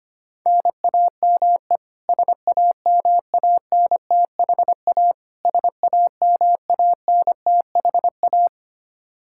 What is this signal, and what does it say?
Morse code, heard as a keyed tone, about 25 wpm, 700 Hz. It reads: NAME SAMANTHA SAMANTHA
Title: Practice QSO - Exchange 1 - Beginning - 25wpm